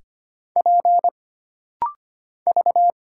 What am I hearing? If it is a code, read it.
Morse code, heard as a keyed tone, about 25 wpm, 700 Hz. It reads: P V
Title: Single Letters - Rapid-Fire 25wpm